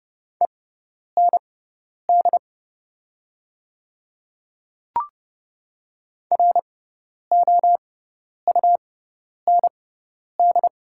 Morse code, heard as a keyed tone, about 30 wpm, spaced as for 10 wpm, 700 Hz. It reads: END ROUND
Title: 6x Character Spacing - Top 200 Words - 30wpm